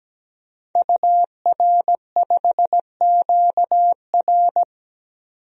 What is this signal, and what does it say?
Morse code, heard as a keyed tone, about 17 wpm, 700 Hz. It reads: UR5QR
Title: DXCC-100 Call Signs - 5 Characters - 17wpm